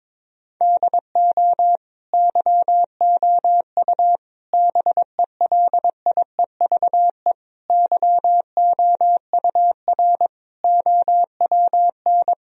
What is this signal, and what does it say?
Morse code, heard as a keyed tone, about 22 wpm, 700 Hz. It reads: DO YOU BELIEVE YOUR OWN
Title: Sentences from Top 1000 Words - Review Part 3 - 22wpm